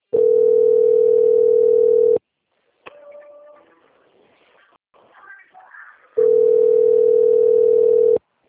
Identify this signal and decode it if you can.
Ringback tone (North America), 440 + 480 Hz.
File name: out- - - - - .44944